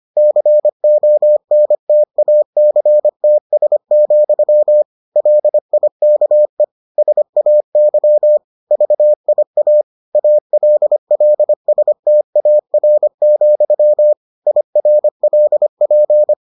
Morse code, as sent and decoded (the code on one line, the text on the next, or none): -.-. --- -. - .- -.-. - ... --..-- / .-.. .. -.- . / ... .- -.-- / ...- .. .- / .- .-.. .-.. ... - .- .-. --..-- / .. .-. .-.. .--.
CONTACTS, LIKE SAY VIA ALLSTAR, IRLP